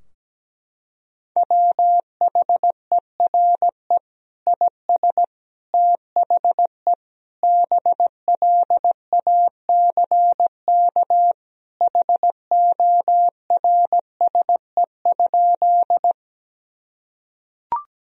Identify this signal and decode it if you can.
Morse code, heard as a keyed tone, about 17 wpm, 700 Hz. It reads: WHERE IS THE BLACK HORSE?
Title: Sentences from Top 1000 Words - Review Part 2 - 17wpm